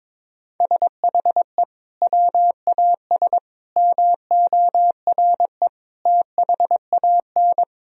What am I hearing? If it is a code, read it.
Morse code, heard as a keyed tone, about 22 wpm, 700 Hz. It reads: SHE WAS MORE THAN
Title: Sentences from Top 1000 Words - Review Part 3 - 22wpm